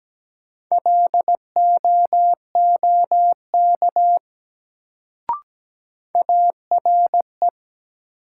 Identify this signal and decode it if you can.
Morse code, heard as a keyed tone, about 17 wpm, 700 Hz. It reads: LOOK ARE
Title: Sentences from Top 100 Words - New Word Drill - 17wpm